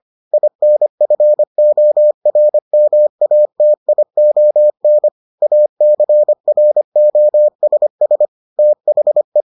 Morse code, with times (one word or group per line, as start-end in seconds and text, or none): 0.3-5.1 INFORMATION
5.4-8.3 ACROSS
8.6-9.4 THE